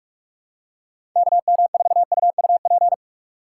Morse code, heard as a keyed tone, about 46 wpm, 700 Hz. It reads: KM4UUP
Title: USA Call Signs - 6 Characters - 4x Character Spacing - 45wpm